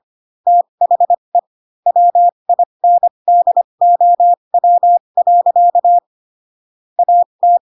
Morse code, heard as a keyed tone, about 25 wpm, spaced as for 22 wpm, 700 Hz. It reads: THE WINDOW. AT
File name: Pinocchio / Pinocchio ch7 18 25 ik1hge